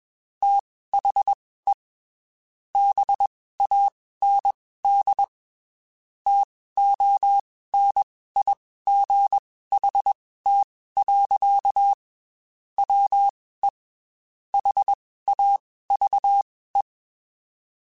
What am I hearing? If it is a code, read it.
Morse code, heard as a keyed tone, about 21 wpm, spaced as for 15 wpm, 800 Hz. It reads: THE BAND TONIGHT. WE HAVE
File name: QSOs / QSOs-mp3s / WD0BGP 1@20